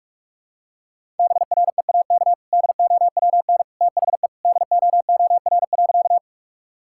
Morse code, heard as a keyed tone, about 44 wpm, 700 Hz. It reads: BREAK DOWN THE DOOR.
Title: Sentences from Top 1000 Words - Review Part 6 - 45wpm